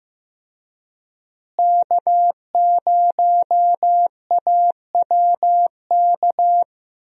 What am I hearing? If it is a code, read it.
Morse code, heard as a keyed tone, about 15 wpm, 700 Hz. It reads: K0AWK
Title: USA Call Signs - 5 Characters - Repeated 3x - 15wpm